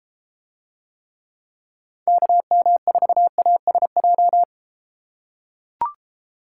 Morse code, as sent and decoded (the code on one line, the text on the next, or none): -.- -- ....- .- ... .---
KM4ASJ